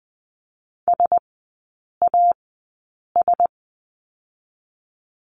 Morse code, heard as a keyed tone, about 20 wpm, 700 Hz. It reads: S A S
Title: 035 Lesson 7e - Letter s - Three Characters 20wpm